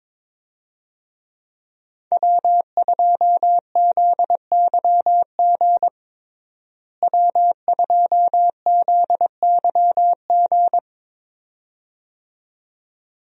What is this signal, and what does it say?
Morse code, heard as a keyed tone, about 22 wpm, 700 Hz. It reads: W2ZYG W2ZYG
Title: USA Call Signs - 5 Characters - Repeated 2x - 22wpm